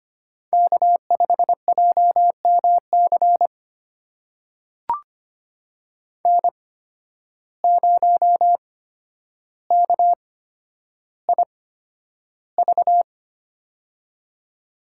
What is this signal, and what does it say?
Morse code, heard as a keyed tone, about 25 wpm, spaced as for 11 wpm, 700 Hz. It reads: K5JMC N0KIV
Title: USA Call Signs - 8x Character Spacing - 25wpm